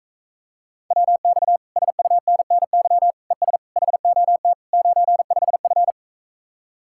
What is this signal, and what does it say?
Morse code, heard as a keyed tone, about 45 wpm, 700 Hz. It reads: WX SUNNY ES HOT 95F